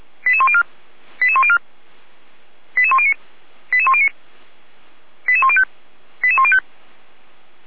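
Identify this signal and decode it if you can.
Five-tone selective call (ZVEI), heard as quick runs of five tones, tones of about 70 ms. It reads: 88104 88104 88109 88109 88105 88105
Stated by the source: SDR-received ZVEI five-tone call, code 88192, archived by feuerwehr-orlamuende (fire brigade Orlamünde) server f85360.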